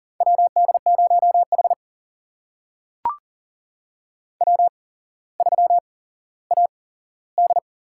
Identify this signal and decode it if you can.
Morse code, heard as a keyed tone, about 41 wpm, spaced as for 17 wpm, 700 Hz. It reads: WD0H W3AD
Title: USA Call Signs - 4 Characters - 8x Character Spacing - 40wpm